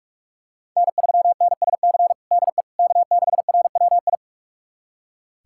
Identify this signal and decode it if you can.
Morse code, heard as a keyed tone, about 46 wpm, 700 Hz. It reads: N3NSC DE K6RWI
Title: USA Callsign DE Callsign - 5 Characters - 45wpm